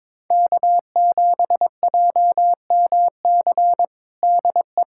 Morse code, heard as a keyed tone, about 22 wpm, 700 Hz. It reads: K7JMC DE
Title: USA Callsign DE Callsign 22wpm